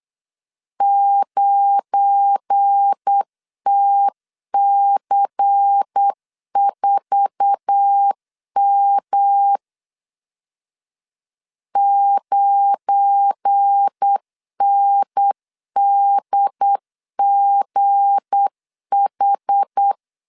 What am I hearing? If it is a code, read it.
Morse code, heard as a keyed tone, about 8 wpm, 800 Hz. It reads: 9TC4M 9NDGH
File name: LEZIONI / 8 YJ45 / 8 YJ4 8wpm 3246